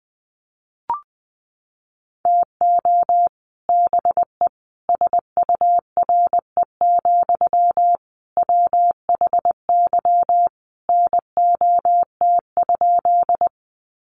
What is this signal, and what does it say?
Morse code, heard as a keyed tone, about 20 wpm, 700 Hz. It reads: TO BE SURE, WHY NOT?
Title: Sentences from Top 600 Words 20wpm